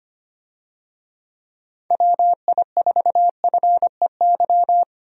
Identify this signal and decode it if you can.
Morse code, heard as a keyed tone, about 25 wpm, 700 Hz. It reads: WI4FEY
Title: USA Call Signs - 6 Characters - 25wpm